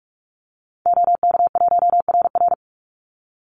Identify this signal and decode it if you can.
Morse code, heard as a keyed tone, about 44 wpm, 700 Hz. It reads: OK1RR
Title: Call Signs - 45wpm